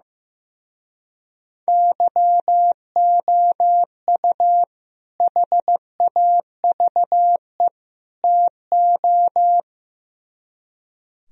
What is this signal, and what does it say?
Morse code, heard as a keyed tone, about 16 wpm, 700 Hz. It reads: YOU HAVE TO
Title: Speed-Racing - Sets of 3 Words 15wpm